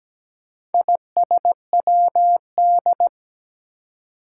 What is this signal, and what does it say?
Morse code, heard as a keyed tone, about 17 wpm, 700 Hz. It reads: ISWD
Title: Four Challenging Letter-Number Combinations 17wpm